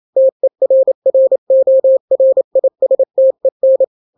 Morse code, heard as a keyed tone, about 28 wpm, 530 Hz. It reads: TERRORISTEN
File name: daily / 27wpm